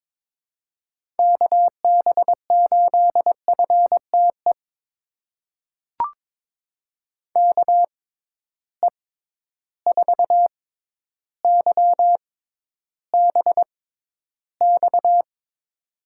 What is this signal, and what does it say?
Morse code, heard as a keyed tone, about 22 wpm, spaced as for 11 wpm, 700 Hz. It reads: KB8FTE KE4YBX
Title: USA Call Signs - 6 Characters - 6x Character Spacing - 22wpm